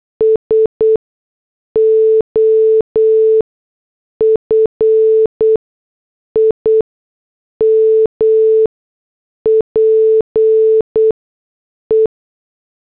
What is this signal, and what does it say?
Morse code, heard as a keyed tone, about 8 wpm, spaced as for 6 wpm, 440 Hz. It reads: S OF IMPE
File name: morse / headlines